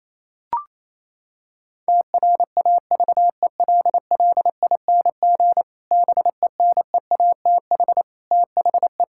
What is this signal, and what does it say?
Morse code, heard as a keyed tone, about 28 wpm, 700 Hz. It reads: TRAVELLING BENEATH THE
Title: Pangram Challenge v2 - 28wpm